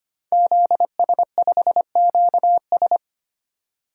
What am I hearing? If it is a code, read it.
Morse code, heard as a keyed tone, about 25 wpm, 700 Hz. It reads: ZS5QS
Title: DXCC-100 Call Signs - 5 Characters - 25wpm